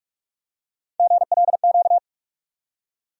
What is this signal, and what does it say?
Morse code, heard as a keyed tone, about 46 wpm, 700 Hz. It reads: GLQ
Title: USA Call Sign Postfixes - 3 Letters 45wpm